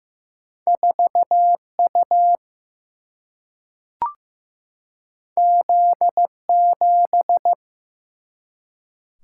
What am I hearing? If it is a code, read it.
Morse code, heard as a keyed tone, about 16 wpm, 700 Hz. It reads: 4U Z7
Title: Single-Letter Single-Number 15wpm